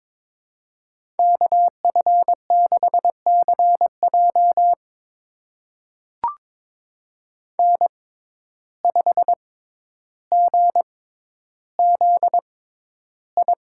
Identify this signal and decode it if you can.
Morse code, heard as a keyed tone, about 22 wpm, spaced as for 11 wpm, 700 Hz. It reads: KF6CJ N5GZI
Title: USA Call Signs - 5 Characters - 6x Character Spacing - 22wpm